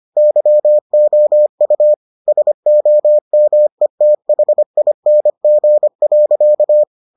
Morse code, heard as a keyed tone, about 25 wpm, 600 Hz. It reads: YOU SOMETHING.